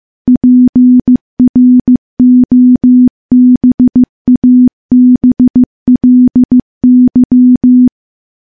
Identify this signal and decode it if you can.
Morse code, heard as a keyed tone, about 15 wpm, 250 Hz. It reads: PROBABLY
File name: morse-training-15WPM